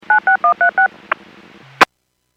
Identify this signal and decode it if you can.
DTMF, heard as telephone touch tones, tones of about 110 ms. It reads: 96136
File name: Businesses / 00311 23-54 Businesses 451.25 WPS